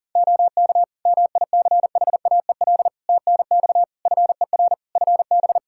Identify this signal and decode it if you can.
Morse code, heard as a keyed tone, about 40 wpm, 700 Hz. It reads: OK MICHAEL TNX FER FB
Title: K1USN SST QSO Practice 40wpm